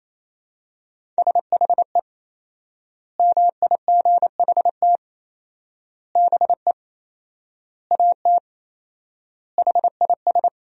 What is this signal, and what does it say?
Morse code, heard as a keyed tone, about 28 wpm, 700 Hz. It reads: SHE MIGHT BE AT HIS
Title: 4x Word-Spacing - Sentences from Top 300 Words - 28wpm